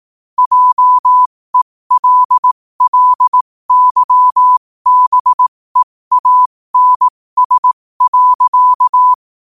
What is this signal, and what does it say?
Morse code, heard as a keyed tone, about 18 wpm, spaced as for 15 wpm, 1000 Hz. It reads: JELLYBEANS.
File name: qotdfiles / 15wpm